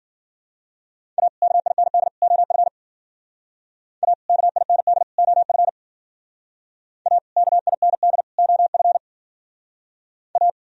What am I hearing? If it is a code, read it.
Morse code, heard as a keyed tone, about 50 wpm, 700 Hz. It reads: A KIND OF A KIND OF A KIND OF A
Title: Speed-Racing - Sets of 3 Words 40wpm